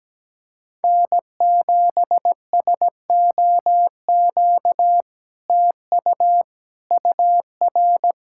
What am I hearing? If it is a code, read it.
Morse code, heard as a keyed tone, about 17 wpm, 700 Hz. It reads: N7SOQ TU UR
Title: SOTA QSO Practice 17wpm